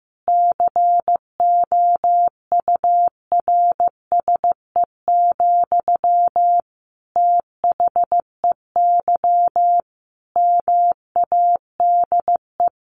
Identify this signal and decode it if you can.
Morse code, heard as a keyed tone, about 15 wpm, 700 Hz. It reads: COURSE, THEY MADE
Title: Sentences from Top 1400 Words 15wpm